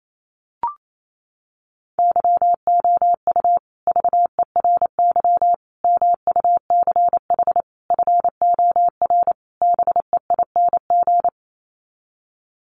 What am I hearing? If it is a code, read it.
Morse code, heard as a keyed tone, about 28 wpm, 690 Hz. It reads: YOU VERY MUCH FOR BEING
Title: Sets of 5 Words Encore 28wpm